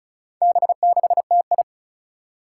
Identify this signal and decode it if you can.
Morse code, heard as a keyed tone, about 36 wpm, 700 Hz. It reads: B6TI
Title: Four Challenging Letter-Number Combinations 35wpm